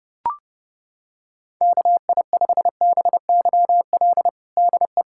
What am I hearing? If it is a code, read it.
Morse code, heard as a keyed tone, about 30 wpm, 700 Hz. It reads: KI5BYL DE
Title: USA Callsign DE Callsign - 6 Characters - 30wpm